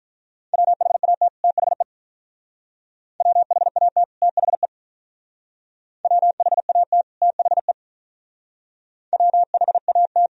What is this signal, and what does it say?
Morse code, heard as a keyed tone, about 45 wpm, 700 Hz. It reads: WHAT THE WHAT THE WHAT THE WHAT